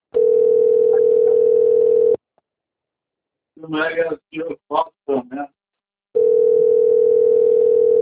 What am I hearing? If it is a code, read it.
Ringback tone (North America), 440 + 480 Hz, with speech.
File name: rg-1600-unknown- - - .56746